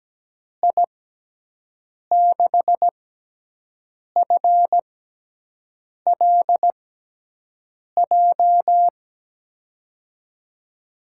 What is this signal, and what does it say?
Morse code, heard as a keyed tone, about 17 wpm, spaced as for 6 wpm, 700 Hz. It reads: I6FLJ
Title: USA Call Signs - 6x Character Spacing - 17wpm